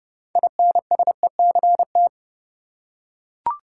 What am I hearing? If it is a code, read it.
Morse code, heard as a keyed tone, about 30 wpm, 700 Hz. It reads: INSECT
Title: Six Letter Words - 30wpm